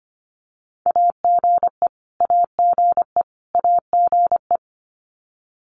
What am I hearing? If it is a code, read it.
Morse code, heard as a keyed tone, about 25 wpm, 700 Hz. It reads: AGE AGE AGE
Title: Top 500 Words Repeated 25wpm